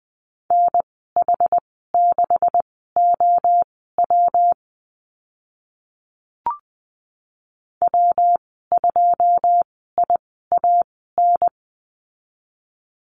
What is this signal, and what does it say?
Morse code, heard as a keyed tone, about 20 wpm, spaced as for 14 wpm, 700 Hz. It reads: NH6OW W2IAN
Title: 128 Lesson 21g - Number 6 - Call Signs 20wpm